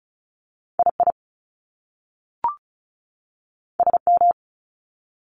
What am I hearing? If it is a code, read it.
Morse code, heard as a keyed tone, about 36 wpm, 700 Hz. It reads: II SM
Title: Top Two Letter Combinations in Words 35wpm